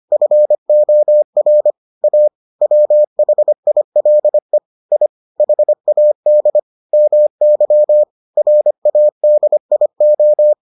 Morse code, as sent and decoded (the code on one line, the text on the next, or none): ..-. --- .-. / .- / .-- .... .. .-.. / .. / .... .- -.. / -- -.-- / .-. .- -.. .. ---
FOR A WHILE I HAD MY RADIO